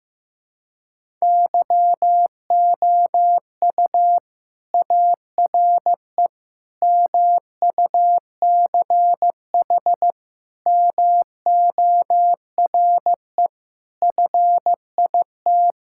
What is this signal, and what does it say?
Morse code, heard as a keyed tone, about 15 wpm, 700 Hz. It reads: YOU ARE MUCH MORE FIT
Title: Sentences from Top 1000 Words - Review Part 5 - 15wpm